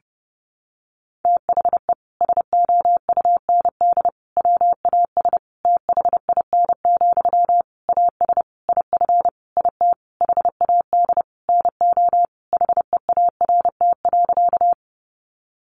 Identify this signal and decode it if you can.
Morse code, heard as a keyed tone, about 30 wpm, 700 Hz. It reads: THE SOUND WAS THIN, AS IF IT HAD NO HEART.
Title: Sentences from Top 1200 Words 30wpm